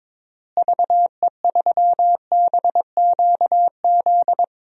Morse code, heard as a keyed tone, about 22 wpm, 700 Hz. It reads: VE3BQZ
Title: DXCC-100 Call Signs - 6 Characters - Repeated 3x - 22wpm